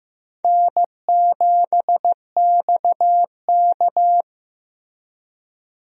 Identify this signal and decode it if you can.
Morse code, heard as a keyed tone, about 15 wpm, 700 Hz. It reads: N7XK
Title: USA Call Signs - 4 Characters - Repeated 3x - 15wpm